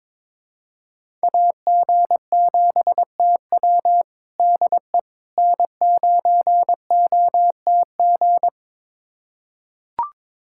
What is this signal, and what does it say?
Morse code, heard as a keyed tone, about 21 wpm, 700 Hz. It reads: AG7TW DE N9OTG
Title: USA Callsign DE Callsign - 5 Characters - 22wpm